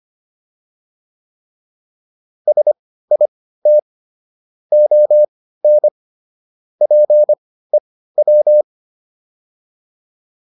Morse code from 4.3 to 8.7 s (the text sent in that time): ON PEW